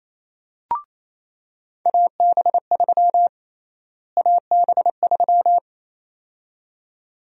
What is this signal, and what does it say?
Morse code, heard as a keyed tone, about 28 wpm, 700 Hz. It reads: AB3 AB3
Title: USA Call Sign Prefixes - Repeated 2x - 28wpm